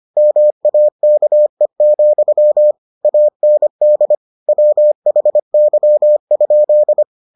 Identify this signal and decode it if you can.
Morse code, heard as a keyed tone, about 25 wpm, 600 Hz. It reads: MAKE, AND WHY?